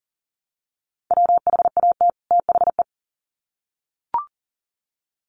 Morse code, heard as a keyed tone, about 40 wpm, 700 Hz. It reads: WHAT THE
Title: Sets of 2 Words 40wpm